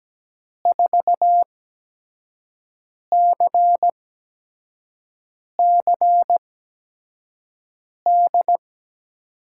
Morse code, heard as a keyed tone, about 17 wpm, spaced as for 5 wpm, 700 Hz. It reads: 4CCD
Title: USA Call Signs - 5 Characters - 8x Character Spacing - 17wpm